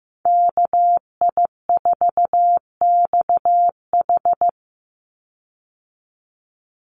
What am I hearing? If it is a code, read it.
Morse code, heard as a keyed tone, about 15 wpm, 700 Hz. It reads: KI4XH
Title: Call Signs Repeated 3x - 15wpm